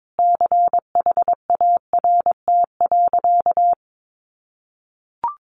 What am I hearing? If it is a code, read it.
Morse code, heard as a keyed tone, about 22 wpm, 700 Hz. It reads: CHART.
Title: Sentences from Top 1300 Words 22wpm